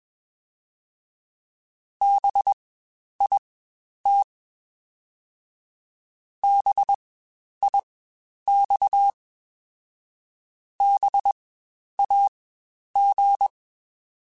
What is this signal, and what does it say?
Morse code, heard as a keyed tone, about 21 wpm, spaced as for 10 wpm, 800 Hz. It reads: BIT BIX BAG
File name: Spaced Alphas / Spaced Alphas mp3s / Bs sp @20